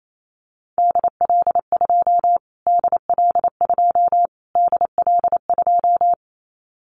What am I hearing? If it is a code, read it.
Morse code, heard as a keyed tone, about 28 wpm, 700 Hz. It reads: DL2 DL2 DL2
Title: Callsign Prefixes for Top 25 DXCC - Repeated 3x - 28wpm